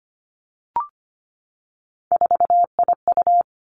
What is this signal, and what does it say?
Morse code, heard as a keyed tone, about 25 wpm, 700 Hz. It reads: 4IU
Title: Three Challenging Letter-Number Combinations 25wpm